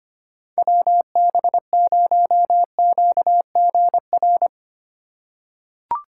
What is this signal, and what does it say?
Morse code, heard as a keyed tone, about 25 wpm, 700 Hz. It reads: WB0QGR
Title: USA Call Signs - 3x Character Spacing - 25wpm